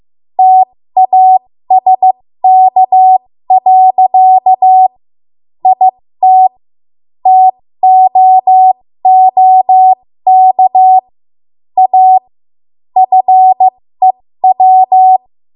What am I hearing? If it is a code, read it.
Morse code, heard as a keyed tone, about 15 wpm, spaced as for 13 wpm, 750 Hz. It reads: TASK. IT TOOK A FEW
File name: Archive / 13 WPM / 13WPM